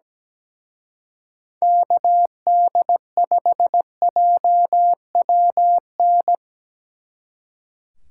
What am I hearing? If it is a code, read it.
Morse code, heard as a keyed tone, about 19 wpm, 700 Hz. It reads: KD5JWN
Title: Speed Racing - USA 2x3 Call Signs 17wpm